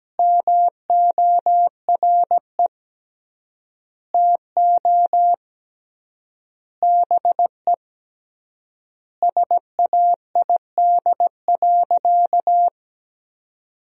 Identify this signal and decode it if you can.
Morse code, heard as a keyed tone, about 17 wpm, 700 Hz. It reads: MORE TO BE SAID.
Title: 3x Word-Spacing - Sentences from Top 100 Words - 17wpm